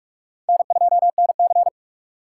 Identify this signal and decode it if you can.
Morse code, heard as a keyed tone, about 46 wpm, 700 Hz. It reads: N2NC
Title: K1USN SST QSO Practice - 45wpm